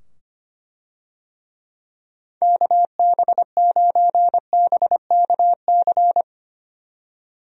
Morse code, heard as a keyed tone, about 25 wpm, 700 Hz. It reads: KB9BKC
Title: USA Call Signs - 6 Characters - 25wpm